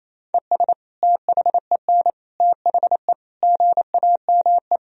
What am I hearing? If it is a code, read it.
Morse code, heard as a keyed tone, about 28 wpm, 700 Hz. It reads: ES THEN THE GAME
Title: Sentences from Top 500 Words and Abbreviations 28wpm